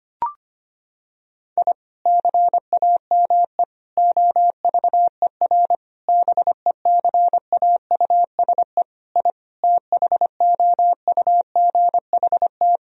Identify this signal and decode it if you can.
Morse code, heard as a keyed tone, about 25 wpm, 700 Hz. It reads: I CAME OVER BECAUSE I THOUGHT
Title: Sentences from Top 1000 Words - Review Part 3 - 25wpm